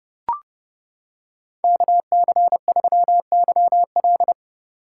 Morse code, heard as a keyed tone, about 28 wpm, 700 Hz. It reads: KC3YL